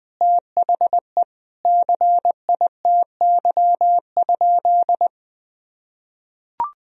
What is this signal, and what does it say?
Morse code, heard as a keyed tone, about 21 wpm, 700 Hz. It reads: THE CITY?
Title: Sentences from Top 1000 Words - Review Part 1 - 20wpm